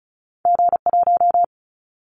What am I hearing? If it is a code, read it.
Morse code, heard as a keyed tone, about 36 wpm, 690 Hz. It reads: G1
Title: Callsign Prefixes for Top 25 DXCC - 35wpm